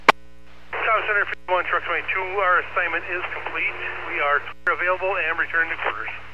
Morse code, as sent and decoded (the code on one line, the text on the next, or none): none